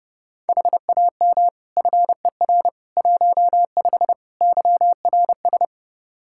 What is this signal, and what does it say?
Morse code, heard as a keyed tone, about 32 wpm, 700 Hz. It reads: HAM FER 15 YRS
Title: Practice QSO - Exchange 3 - More Info - 30wpm